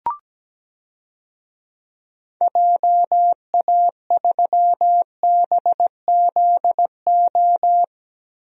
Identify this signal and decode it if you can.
Morse code, heard as a keyed tone, about 17 wpm, 700 Hz. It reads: JA3BZO
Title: DXCC-100 Call Signs - 6 Characters - 17wpm